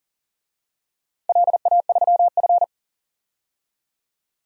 Morse code, heard as a keyed tone, about 40 wpm, 700 Hz. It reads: LA3F
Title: DXCC-100 Call Signs - 4 Characters - 40wpm